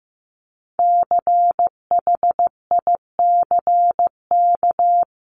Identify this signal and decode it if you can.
Morse code, heard as a keyed tone, about 15 wpm, 700 Hz. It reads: CHICK
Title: CWT Exchange 15wpm